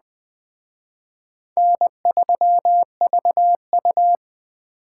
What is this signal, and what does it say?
Morse code, heard as a keyed tone, about 20 wpm, 700 Hz. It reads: N3VU
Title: Speed Racing - USA 1x2 Call Signs 17wpm